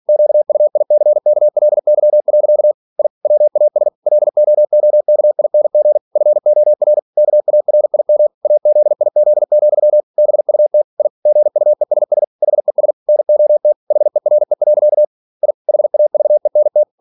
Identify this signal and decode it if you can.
Morse code, heard as a keyed tone, about 46 wpm, 600 Hz. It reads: QUICKLY. I WAS LOOKING FOR KARIM AZIZ, BUT I GUESS HES NOT HERE. I HAVENT